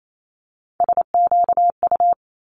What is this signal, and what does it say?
Morse code, heard as a keyed tone, about 28 wpm, 700 Hz. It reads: SQU
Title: Multi-letter Phonemes Quick Fire 28wpm